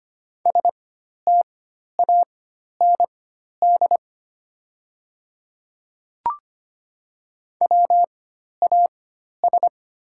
Morse code, heard as a keyed tone, about 25 wpm, spaced as for 12 wpm, 700 Hz. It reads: STAND WAS
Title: 4x Character Spacing - Top 300 Words - 25wpm